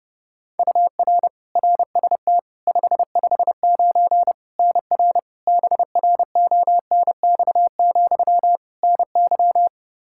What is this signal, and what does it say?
Morse code, heard as a keyed tone, about 31 wpm, 700 Hz. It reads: UR RST 559 NR BRONX, NY
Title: Practice QSO - Exchange 1 - Beginning - 30wpm